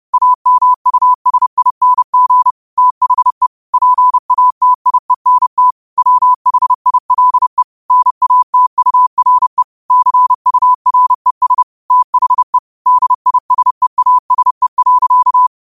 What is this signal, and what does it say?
Morse code, heard as a keyed tone, about 30 wpm, 1000 Hz. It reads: AMUSING THE PATIENT WHILE NATURE CURES THE DISEASE.